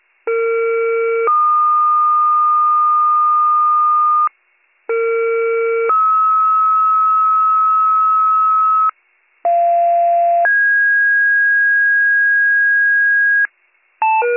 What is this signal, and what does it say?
Two-tone page, heard as a one-second tone, then a three-second tone, about 460 then 1220 Hz, about 460 then 1290 Hz, about 690 then 1650 Hz, with noise. It is heard from 0.3 to 13.5 s.